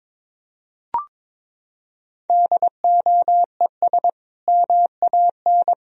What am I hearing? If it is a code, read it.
Morse code, heard as a keyed tone, about 21 wpm, 700 Hz. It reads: DOES MAN
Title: Sentences from Top 1000 Words - Review Part 3 - 22wpm